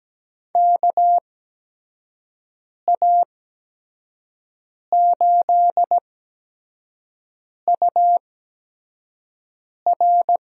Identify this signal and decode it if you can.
Morse code, heard as a keyed tone, about 17 wpm, spaced as for 5 wpm, 700 Hz. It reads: KA8UR
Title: USA Call Signs - 6 Characters - 8x Character Spacing - 17wpm